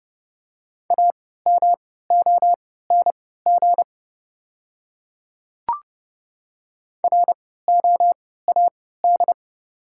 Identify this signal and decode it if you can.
Morse code, heard as a keyed tone, about 29 wpm, spaced as for 17 wpm, 700 Hz. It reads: AMONG ROAD
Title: 3x Character Spacing - Top 500 Words - 30wpm